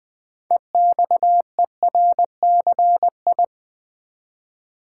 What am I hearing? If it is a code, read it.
Morse code, heard as a keyed tone, about 20 wpm, 700 Hz. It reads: EXERCI
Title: Build A Word - 3 Letters At A Time - 8-Letter Words - 20wpm